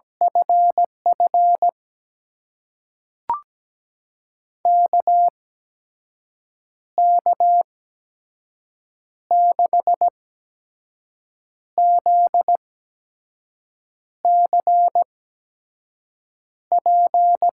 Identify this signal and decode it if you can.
Morse code, heard as a keyed tone, about 17 wpm, spaced as for 5 wpm, 700 Hz. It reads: FF KK6ZCP
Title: USA Call Signs - 6 Characters - 8x Character Spacing - 17wpm